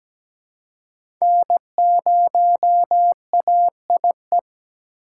Morse code, heard as a keyed tone, about 17 wpm, 700 Hz. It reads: N0AIE
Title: K1USN SST QSO Practice - 17wpm